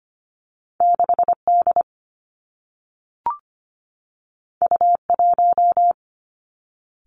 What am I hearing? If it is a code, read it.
Morse code, heard as a keyed tone, about 25 wpm, 700 Hz. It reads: 6 D U 1
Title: Two Challenging Letter-Number Combinations 25wpm